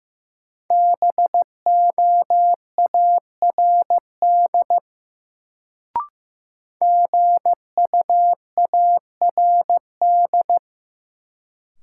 Morse code, heard as a keyed tone, about 15 wpm, 700 Hz. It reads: BOARD GUARD
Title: Sentences from Top 700 Words - New Word Drill - 15wpm